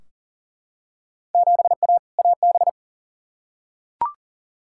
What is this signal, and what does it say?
Morse code, heard as a keyed tone, about 40 wpm, 700 Hz. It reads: 7A AB
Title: Field Day Exchanges Encore - 40wpm